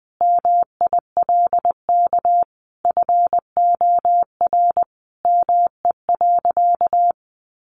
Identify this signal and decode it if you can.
Morse code, heard as a keyed tone, about 20 wpm, 700 Hz. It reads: MILK FOR ME.
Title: Sentences from Top 1500 Words 20wpm